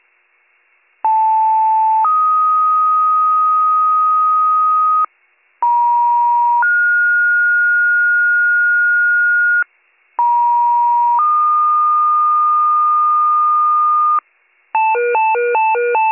Two-tone page, heard as a one-second tone, then a three-second tone, about 880 then 1250 Hz, about 950 then 1430 Hz, about 950 then 1220 Hz, with noise. It is heard from 1.0 to 14.2 s.